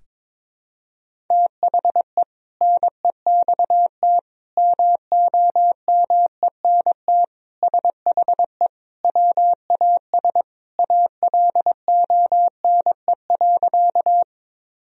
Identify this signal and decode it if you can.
Morse code, heard as a keyed tone, about 21 wpm, 700 Hz. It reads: THE NEXT MOMENT SHE WAS ALONE.
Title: Sentences from Top 1000 Words - Review Part 1 - 22wpm